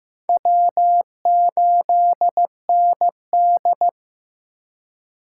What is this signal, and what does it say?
Morse code, heard as a keyed tone, about 15 wpm, 700 Hz. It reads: W8ND
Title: USA Call Signs - 8x Character Spacing - 15wpm